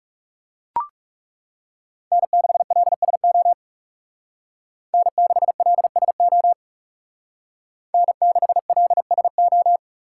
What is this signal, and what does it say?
Morse code, heard as a keyed tone, about 36 wpm, 700 Hz. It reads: N6LSO N6LSO N6LSO
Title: Speed Racing - USA 1x3 Call Signs 30wpm